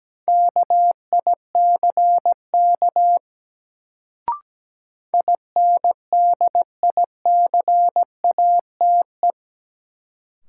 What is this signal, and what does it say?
Morse code, heard as a keyed tone, about 17 wpm, 700 Hz. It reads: KICK INDICATE
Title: Sentences from Top 1300 Words - New Word Drill Alt - 17wpm